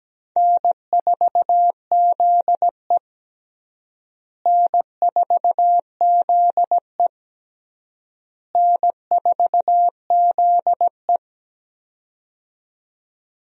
Morse code, heard as a keyed tone, about 17 wpm, 700 Hz. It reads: N4ZE N4ZE N4ZE
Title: USA Call Signs - 4 Characters - Repeated 3x - 17wpm